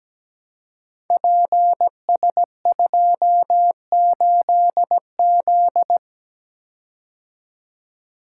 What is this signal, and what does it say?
Morse code, heard as a keyed tone, about 17 wpm, 700 Hz. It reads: PS28Z
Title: Code Groups - 5 Letter:Numbers - 17wpm